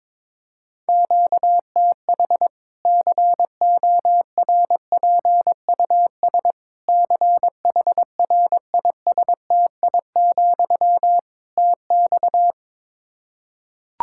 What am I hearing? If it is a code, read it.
Morse code, heard as a keyed tone, about 22 wpm, 700 Hz. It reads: QTH CORPUS CHRISTI, TX
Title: K1USN SST QSO Practice 22wpm